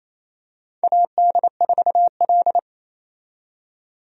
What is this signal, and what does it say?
Morse code, heard as a keyed tone, about 27 wpm, 700 Hz. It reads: AD4L
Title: USA Call Signs - 4 Characters - 28wpm